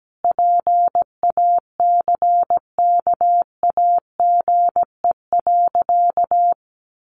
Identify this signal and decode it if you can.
Morse code, heard as a keyed tone, about 17 wpm, 700 Hz. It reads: PACKAGE.
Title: Additional Easy Phrases 17wpm